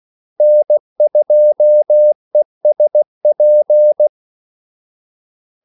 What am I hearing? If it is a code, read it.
Morse code, heard as a keyed tone, about 16 wpm, 600 Hz. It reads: N2ESP
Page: Kaldesignaler 16 wpm